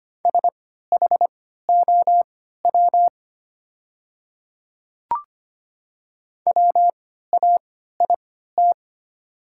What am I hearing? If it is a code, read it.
Morse code, heard as a keyed tone, about 25 wpm, spaced as for 14 wpm, 700 Hz. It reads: SHOW WAIT
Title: 3x Character Spacing - Top 500 Words - 25wpm